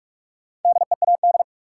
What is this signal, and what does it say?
Morse code, heard as a keyed tone, about 46 wpm, 700 Hz. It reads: DEAD
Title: Sentences from Top 1500 Words - New Word Drill Alt - 45wpm